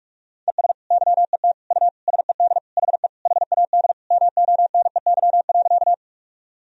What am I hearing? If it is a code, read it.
Morse code, heard as a keyed tone, about 46 wpm, 700 Hz. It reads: ES YET U SED HE HAD MONEY.
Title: Sentences from Top 500 Words and Abbreviations 45wpm